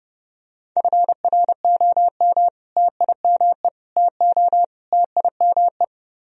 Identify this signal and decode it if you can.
Morse code, heard as a keyed tone, about 30 wpm, 700 Hz. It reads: FROM TIME TO TIME